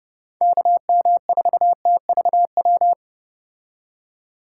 Morse code, heard as a keyed tone, about 29 wpm, 700 Hz. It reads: KM4TVW